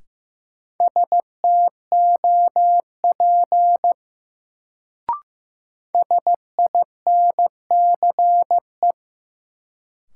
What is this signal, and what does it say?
Morse code, heard as a keyed tone, about 15 wpm, 700 Hz. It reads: STOP SINCE
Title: Sentences from Top 300 Words - New Word Drill - 15wpm